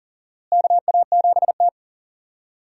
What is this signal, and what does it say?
Morse code, heard as a keyed tone, about 40 wpm, 700 Hz. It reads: KA7T
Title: USA Call Signs - 4 Characters - Repeated 2x - 40wpm